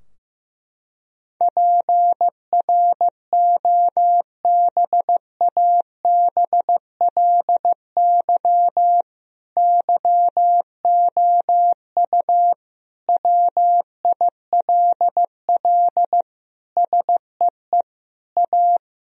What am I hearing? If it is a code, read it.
Morse code, heard as a keyed tone, about 16 wpm, 700 Hz. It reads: PROBABLY YOU WILL SEE A
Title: Sentences from Top 1000 Words - Review Part 4 - 15wpm